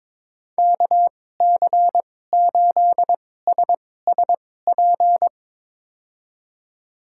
Morse code, heard as a keyed tone, about 22 wpm, spaced as for 16 wpm, 700 Hz. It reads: KC8SSP
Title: USA Call Signs - 6 Characters - 2x Character Spacing - 22wpm